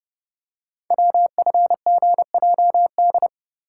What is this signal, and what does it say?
Morse code, heard as a keyed tone, about 30 wpm, 700 Hz. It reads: WFGJD
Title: Code Groups - 5 Letter:Numbers - 30wpm